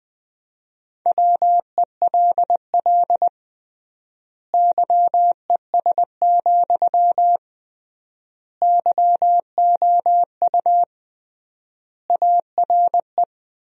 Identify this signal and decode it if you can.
Morse code, heard as a keyed tone, about 21 wpm, 700 Hz. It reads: WELL YES, YOU ARE
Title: 3x Word-Spacing - Sentences from Top 500 Words - 20wpm